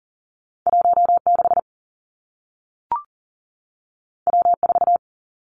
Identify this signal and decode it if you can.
Morse code, heard as a keyed tone, about 40 wpm, 700 Hz. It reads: 1 6 W 4
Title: Two Challenging Letter-Number Combinations 40wpm